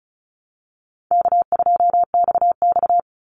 Morse code, heard as a keyed tone, about 34 wpm, 700 Hz. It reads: K2XX
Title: Easy Call Signs Repeated 3x - 35wpm